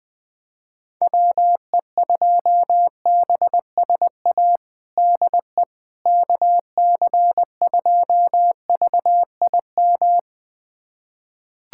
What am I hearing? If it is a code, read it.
Morse code, heard as a keyed tone, about 20 wpm, 700 Hz. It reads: WE2BSA DE KC2VIM
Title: USA Callsign DE Callsign - 6 Characters - 20wpm